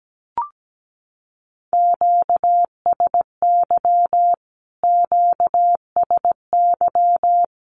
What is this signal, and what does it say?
Morse code, heard as a keyed tone, about 17 wpm, 700 Hz. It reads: QSY QSY
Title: Q-Codes Repeated 3x 17wpm